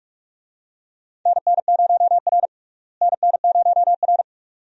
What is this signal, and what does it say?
Morse code, heard as a keyed tone, about 47 wpm, 700 Hz. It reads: NN0R NN0R
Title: USA Call Signs - 4 Characters - Repeated 2x - 45wpm